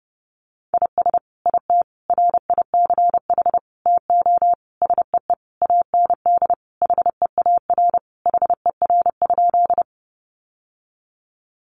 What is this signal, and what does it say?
Morse code, heard as a keyed tone, about 30 wpm, 700 Hz. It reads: IS IT RICH TO SEE AND HEAR HER?
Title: Sentences from Top 1100 Words 30wpm